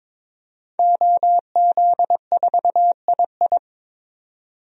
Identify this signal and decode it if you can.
Morse code, heard as a keyed tone, about 23 wpm, 700 Hz. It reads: OZ4II
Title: DXCC-100 Call Signs - 5 Characters - 22wpm